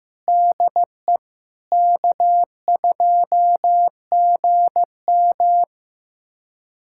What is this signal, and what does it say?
Morse code, heard as a keyed tone, about 15 wpm, 700 Hz. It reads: DE K2GM
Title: USA Callsign DE Callsign 15wpm